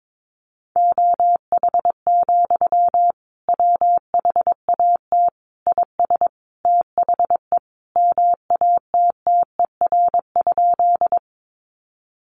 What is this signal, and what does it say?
Morse code, heard as a keyed tone, about 22 wpm, 700 Hz. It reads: OH, WHAT IS THE MATTER?
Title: Sentences from Top 700 Words 22wpm